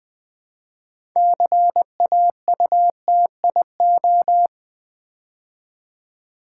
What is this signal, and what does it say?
Morse code, heard as a keyed tone, about 20 wpm, 700 Hz. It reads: CAUTIO
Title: Build A Word - 2 Letters At A Time - 8-Letter Words - 20wpm